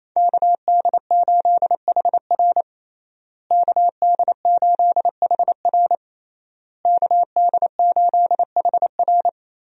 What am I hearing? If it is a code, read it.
Morse code, heard as a keyed tone, about 28 wpm, 700 Hz. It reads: KD8HR KD8HR KD8HR
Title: USA Call Signs - 5 Characters - Repeated 3x - 28wpm